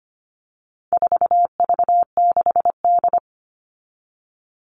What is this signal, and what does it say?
Morse code, heard as a keyed tone, about 25 wpm, 700 Hz. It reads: 4V6D
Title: All Cut Numbers - 4-Digit 25wpm